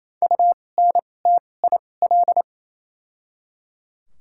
Morse code, heard as a keyed tone, about 28 wpm, spaced as for 20 wpm, 700 Hz. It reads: UNTIL
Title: 2x Character Spacing - Top 300 Words - 28wpm